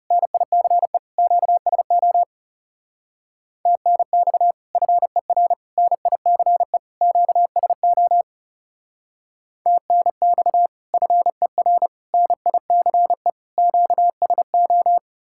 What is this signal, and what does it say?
Morse code, heard as a keyed tone, about 37 wpm, 700 Hz. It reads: NICE QSO TNX FER NICE QSO TNX FER NICE QSO
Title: Speed-Racing Common QSO Phrases Encore - 30wpm